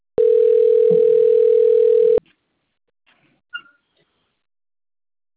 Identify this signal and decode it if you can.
Ringback tone (North America), 440 + 480 Hz.